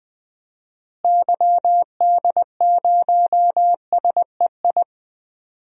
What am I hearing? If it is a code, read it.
Morse code, heard as a keyed tone, about 20 wpm, 700 Hz. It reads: YD0SEI
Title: DXCC-100 Call Signs - 6 Characters - Repeated 2x - 20wpm